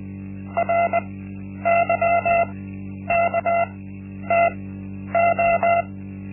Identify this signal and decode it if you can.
Morse code, heard as a keyed tone, about 20 wpm, spaced as for 10 wpm, 680 Hz. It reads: RYKTO
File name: Linksmakalnis-1